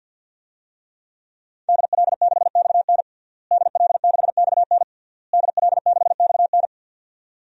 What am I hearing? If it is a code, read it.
Morse code, heard as a keyed tone, about 49 wpm, 700 Hz. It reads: DL6XN DL6XN DL6XN
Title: DXCC-100 Call Signs - 5 Characters - Repeated 3x - 50wpm